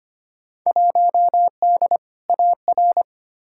Morse code, heard as a keyed tone, about 25 wpm, 700 Hz. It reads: 1D AR
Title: Field Day Exchanges - 25wpm